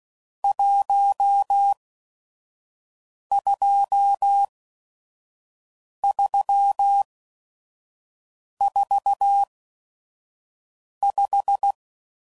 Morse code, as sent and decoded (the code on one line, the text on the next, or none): .---- ..--- ...-- ....- .....
12345